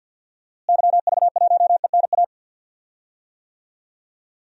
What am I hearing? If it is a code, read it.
Morse code, heard as a keyed tone, about 53 wpm, 700 Hz. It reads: YV1ENA
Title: DXCC-100 Call Signs - 6 Characters - Repeated 2x - 50wpm